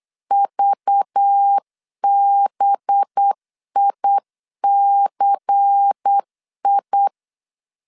Morse code, heard as a keyed tone, about 8 wpm, 800 Hz. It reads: VBICI
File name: CW / DET / 1 6 1500C S C 8wpm 5219